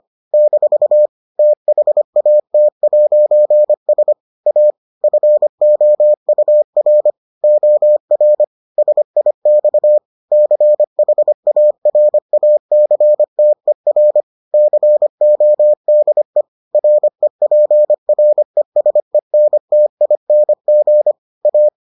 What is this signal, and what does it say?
Morse code, heard as a keyed tone, about 25 wpm, 600 Hz. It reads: - THAT'S A FOUR OR SIX CHARACTER CODE REPRESENTING A